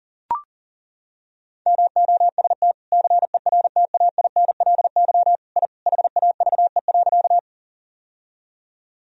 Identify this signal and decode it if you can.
Morse code, heard as a keyed tone, about 37 wpm, 700 Hz. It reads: MOST CERTAINLY I HAVE.
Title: Sentences from Top 1000 Words - Review Part 1 - 40wpm